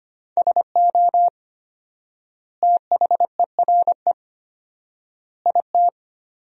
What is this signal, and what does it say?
Morse code, heard as a keyed tone, about 25 wpm, 700 Hz. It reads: SO THERE IT
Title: 4x Word-Spacing - Sentences from Top 100 Words - 25wpm